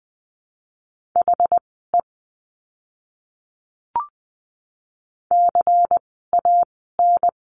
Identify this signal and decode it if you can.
Morse code, heard as a keyed tone, about 20 wpm, spaced as for 14 wpm, 700 Hz. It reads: HE CAN
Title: 192 Lesson 31f - Letter v - Words 20wpm